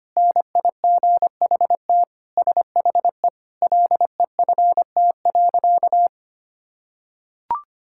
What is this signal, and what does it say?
Morse code, heard as a keyed tone, about 25 wpm, 700 Hz. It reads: NIGHT SHE LEFT.
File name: practice / 25wpm / Sentences with 3-Word Phrases - 25wpm